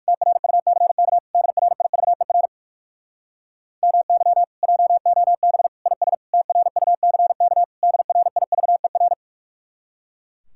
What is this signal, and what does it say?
Morse code, heard as a keyed tone, about 59 wpm, 700 Hz. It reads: TRUCK DRIVER MY JOB IS TRUCK DRIVER